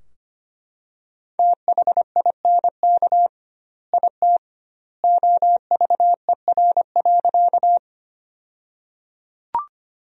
Morse code, heard as a keyed tone, about 25 wpm, 700 Hz. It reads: THINK IT OVER.
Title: 2x Word-Spacing - Sentences from Top 200 Words - 25wpm